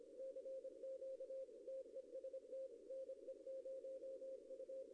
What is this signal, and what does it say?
Morse code, heard as a keyed tone, about 27 wpm, 530 Hz. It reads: CQ TEST NE0U
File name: contests / arrl-160-k5tr-op-w5tn -0100